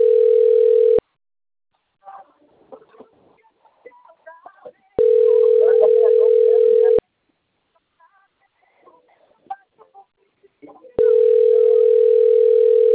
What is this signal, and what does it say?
Ringback tone (North America), 440 + 480 Hz.